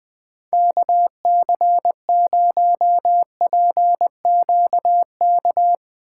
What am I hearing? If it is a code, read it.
Morse code, heard as a keyed tone, about 19 wpm, 700 Hz. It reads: KC0PQK
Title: Speed Racing - USA Call Signs 20wpm